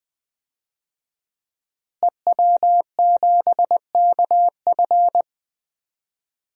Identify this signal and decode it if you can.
Morse code, heard as a keyed tone, about 20 wpm, 700 Hz. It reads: EW7KF
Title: DXCC-100 Call Signs - 5 Characters - Repeated 2x - 20wpm